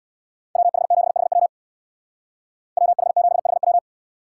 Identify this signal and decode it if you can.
Morse code, heard as a keyed tone, about 62 wpm, 700 Hz. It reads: WH7HW WH7HW
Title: Speed Racing - USA 2x2 Call Signs 50wpm